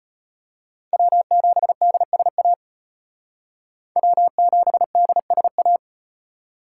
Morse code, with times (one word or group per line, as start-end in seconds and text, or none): 0.9-2.5 W7DSA
4.0-5.8 W7DSA